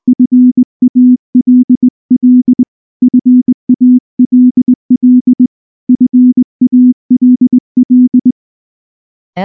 Morse code, heard as a keyed tone, about 19 wpm, 250 Hz. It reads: FALL FALL FALL
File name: morse / morse-training-20WPM-5